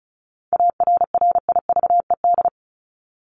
Morse code, heard as a keyed tone, about 34 wpm, 700 Hz. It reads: ARRIVED
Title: ED Suffix - Top 100 Words with Base Words 35wpm